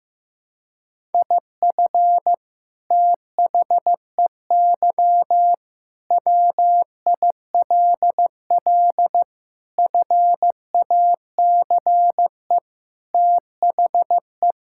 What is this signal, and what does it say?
Morse code, heard as a keyed tone, about 15 wpm, 700 Hz. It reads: IF THEY WILL FACE THE
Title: Sentences from Top 1000 Words - Review Part 6 - 15wpm